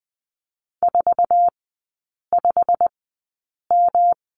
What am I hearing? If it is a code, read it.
Morse code, heard as a keyed tone, about 20 wpm, spaced as for 8 wpm, 700 Hz. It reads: 45M
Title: 114 Lesson 19e - Letter w - Three Characters 20wpm